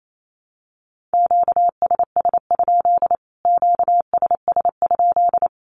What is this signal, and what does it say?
Morse code, heard as a keyed tone, about 28 wpm, 700 Hz. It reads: QSS? QSS?
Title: Q-Codes Repeated 3x 28wpm